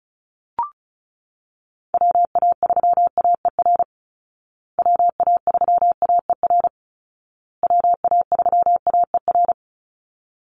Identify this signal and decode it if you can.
Morse code, heard as a keyed tone, about 34 wpm, 700 Hz. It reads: WA3AER WA3AER WA3AER
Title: Call Signs Repeated 2x - 35wpm